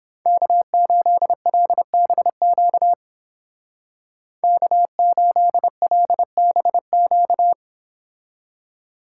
Morse code, heard as a keyed tone, about 29 wpm, 700 Hz. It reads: K8LBQ K8LBQ